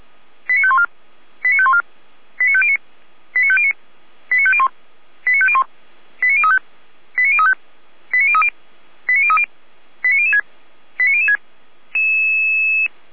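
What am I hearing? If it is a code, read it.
Five-tone selective call (ZVEI), heard as quick runs of five tones, tones of about 70 ms. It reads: 89514 89514 89509 89509 89501 89501 89035 89035 89030 89030 89006 89006
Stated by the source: ZVEI five-tone selective call received by SDR.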